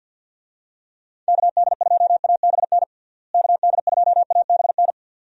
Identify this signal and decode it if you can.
Morse code, heard as a keyed tone, about 51 wpm, 700 Hz. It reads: KD2ABN KD2ABN
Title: USA Call Signs - 6 Characters - Repeated 2x - 50wpm